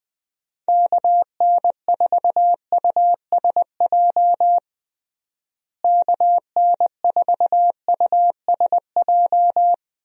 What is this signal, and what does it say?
Morse code, heard as a keyed tone, about 21 wpm, 700 Hz. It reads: KN4USJ KN4USJ
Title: USA Call Signs - 6 Characters - Repeated 3x - 20wpm